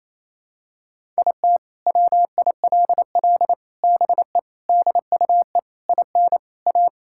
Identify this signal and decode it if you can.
Morse code, heard as a keyed tone, about 28 wpm, 700 Hz. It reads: IT WILL BE DUE IN A
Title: Sentences from Top 1000 Words - Review Part 6 - 28wpm